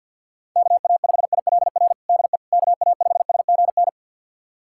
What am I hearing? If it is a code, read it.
Morse code, heard as a keyed tone, about 52 wpm, 700 Hz. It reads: KA5ILR DE KA5SGN
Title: USA Callsign DE Callsign - 6 Characters - 50wpm